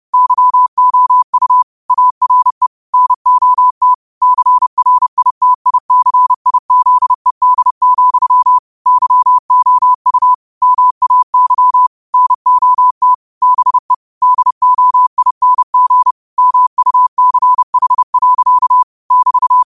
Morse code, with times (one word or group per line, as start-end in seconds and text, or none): 0.1-1.6 YOU
1.9-2.7 ARE
2.9-3.9 NOT
4.2-8.6 CRITICIZED,
8.9-10.3 YOU
10.6-11.9 MAY
12.1-13.1 NOT
13.4-13.9 BE
14.2-16.1 DOING
16.4-18.8 MUCH.
19.1-19.6 <BT>